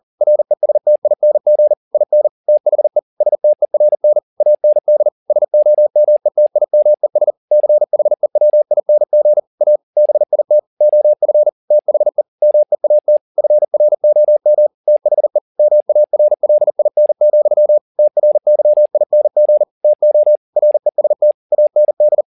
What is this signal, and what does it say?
Morse code, heard as a keyed tone, about 40 wpm, 600 Hz. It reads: RESTING IN THE STERN AND SOMETIMES CHEWING A BIT OF THE MEAT FROM THE MARLIN, TRYING TO REST AND
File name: the old man and the sea 2 40wpm